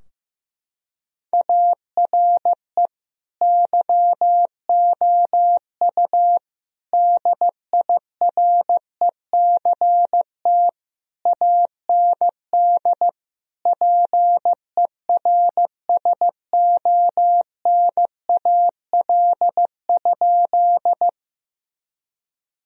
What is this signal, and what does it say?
Morse code, heard as a keyed tone, about 15 wpm, 700 Hz. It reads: ARE YOU DIRECT AND PERSONAL?
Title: Sentences from Top 1000 Words - Review Part 4 - 15wpm